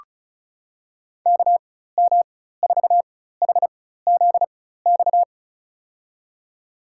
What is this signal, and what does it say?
Morse code, heard as a keyed tone, about 36 wpm, spaced as for 16 wpm, 700 Hz. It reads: KM4HZX
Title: USA Call Signs - 6 Characters - 4x Character Spacing - 35wpm